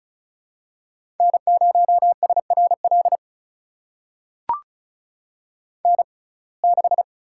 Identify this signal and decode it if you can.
Morse code, heard as a keyed tone, about 36 wpm, spaced as for 26 wpm, 700 Hz. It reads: N0SRL N6
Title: USA Call Signs - 5 Characters - 6x Character Spacing - 35wpm